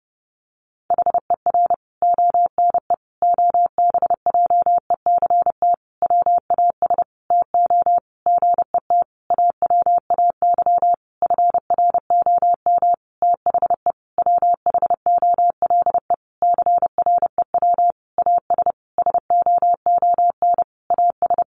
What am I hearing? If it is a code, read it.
Morse code, heard as a keyed tone, about 28 wpm, 700 Hz. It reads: HER ONE OBJECT WAS TO GET AWAY FROM THE WHOLE CREW AS SOON AS
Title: Sentences from Top 1500 Words 30wpm